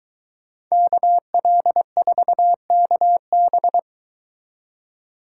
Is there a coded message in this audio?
Yes: Morse code from 0.7 to 3.8 s, heard as a keyed tone, about 23 wpm, 700 Hz. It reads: KL4KB